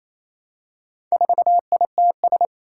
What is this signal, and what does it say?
Morse code, heard as a keyed tone, about 27 wpm, 700 Hz. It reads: 4ITS
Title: Four Challenging Letter-Number Combinations 28wpm